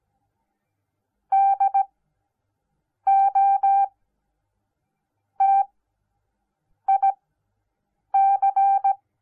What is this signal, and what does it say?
Morse code, heard as a keyed tone, about 17 wpm, spaced as for 6 wpm, 780 Hz. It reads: DOTIC